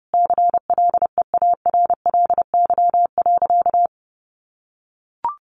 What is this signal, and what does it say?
Morse code, heard as a keyed tone, about 30 wpm, 700 Hz. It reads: CLEARLY.
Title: Sentences from Top 1300 Words 30wpm